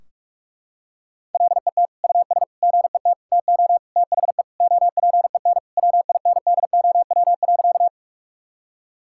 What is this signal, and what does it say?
Morse code, heard as a keyed tone, about 42 wpm, 700 Hz. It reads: LET US GET TO THE OPEN WINDOW.
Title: Sentences from Top 1600 Words 45wpm